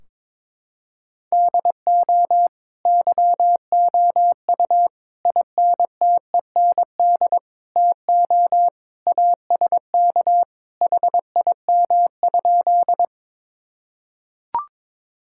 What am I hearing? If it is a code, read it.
Morse code, heard as a keyed tone, about 22 wpm, 700 Hz. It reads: DO YOU INTEND TO ASK HIM?
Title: Sentences from Top 1600 Words 22wpm